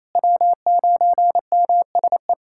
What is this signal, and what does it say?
Morse code, heard as a keyed tone, about 27 wpm, 700 Hz. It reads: W9MSE
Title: USA Callsign DE Callsign - 5 Characters - 28wpm